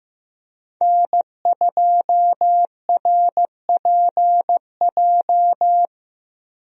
Morse code, heard as a keyed tone, about 15 wpm, 700 Hz. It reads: N2RPJ
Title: USA Call Signs - 5 Characters - 15wpm